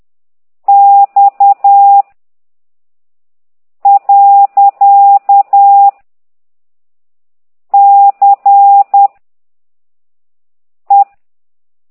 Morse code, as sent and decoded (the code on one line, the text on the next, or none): -..- / .-.-.- / -.-. / .
X . C E